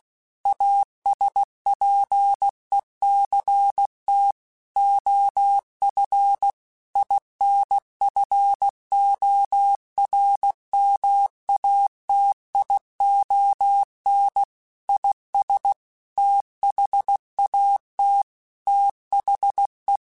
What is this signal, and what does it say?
Morse code, heard as a keyed tone, about 16 wpm, 800 Hz. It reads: ASPECT OF INFORMATION IS THAT THE
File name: Technical Subjects / Technical Subjects-mp3s / Sw Pwr Sup @15